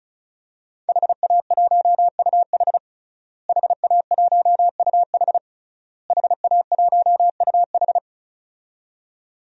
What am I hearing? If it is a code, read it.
Morse code, heard as a keyed tone, about 34 wpm, 700 Hz. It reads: HA1UH HA1UH HA1UH
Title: DXCC-100 Call Signs - 5 Characters - Repeated 3x - 35wpm